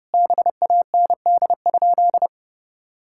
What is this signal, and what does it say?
Morse code, heard as a keyed tone, about 30 wpm, 700 Hz. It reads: BAND?
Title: Sentences from Top 1600 Words 30wpm